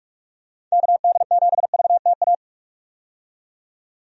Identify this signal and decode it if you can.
Morse code, heard as a keyed tone, about 43 wpm, 700 Hz. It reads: KD7VTA
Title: USA Call Signs - 6 Characters - 45wpm